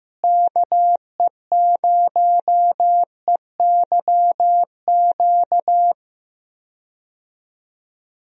Morse code, heard as a keyed tone, about 15 wpm, 700 Hz. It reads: KE0EYQ
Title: USA Call Signs - 6 Characters - Repeated 2x - 15wpm